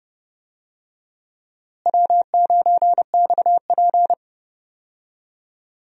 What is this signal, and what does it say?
Morse code, heard as a keyed tone, about 30 wpm, 700 Hz. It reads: W9XP
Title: USA Call Signs - 4 Characters - Repeated 3x - 30wpm